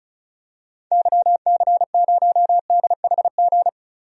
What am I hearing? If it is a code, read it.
Morse code, heard as a keyed tone, about 34 wpm, 700 Hz. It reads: YC0DHG
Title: DXCC-100 Call Signs - 6 Characters - 35wpm